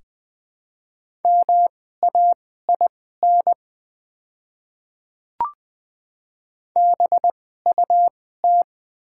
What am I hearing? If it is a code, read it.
Morse code, heard as a keyed tone, about 20 wpm, spaced as for 14 wpm, 700 Hz. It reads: MAIN BUT
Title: 2x Character Spacing - Top 300 Words - 20wpm